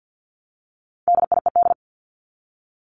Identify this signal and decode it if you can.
Morse code, heard as a keyed tone, about 48 wpm, 700 Hz. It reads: DIED
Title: ED Suffix - Top 100 Words with Base Words 50wpm